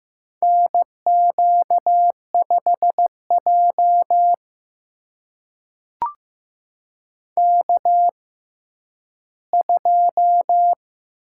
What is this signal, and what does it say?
Morse code, heard as a keyed tone, about 15 wpm, spaced as for 10 wpm, 700 Hz. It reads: NQ5J K2
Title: USA Call Signs - 4 Characters - 6x Character Spacing - 15wpm